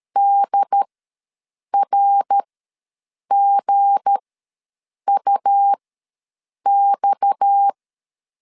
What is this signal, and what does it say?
Morse code, heard as a keyed tone, about 13 wpm, spaced as for 7 wpm, 800 Hz. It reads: DRGUX
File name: LEZIONI / 8 YJ45 / 8 YJ ws 12wpm 3342